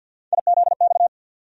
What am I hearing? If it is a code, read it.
Morse code, heard as a keyed tone, about 48 wpm, 700 Hz. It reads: IZX
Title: USA Call Sign Postfixes - 3 Letters 50wpm